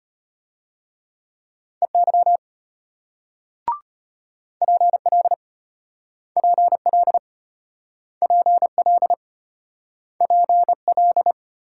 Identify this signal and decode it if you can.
Morse code, heard as a keyed tone, about 34 wpm, 700 Hz. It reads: EY PL PL PL PL